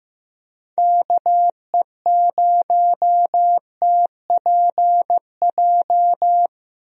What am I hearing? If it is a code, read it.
Morse code, heard as a keyed tone, about 15 wpm, 700 Hz. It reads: KE0TPJ
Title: USA Call Signs - 6 Characters - Repeated 2x - 15wpm